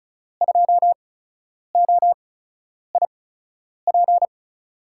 Morse code, heard as a keyed tone, about 36 wpm, spaced as for 10 wpm, 700 Hz. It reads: 2OIP
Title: USA Call Signs - 8x Character Spacing - 35wpm